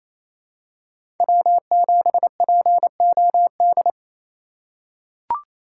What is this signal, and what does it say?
Morse code, heard as a keyed tone, about 28 wpm, 700 Hz. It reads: W7POD
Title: USA Call Signs - 3x Character Spacing - 28wpm